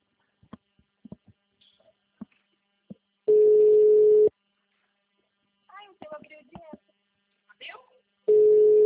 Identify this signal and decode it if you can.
Ringback tone (Europe, 425 Hz), with speech.
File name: out- -2033- - - .1365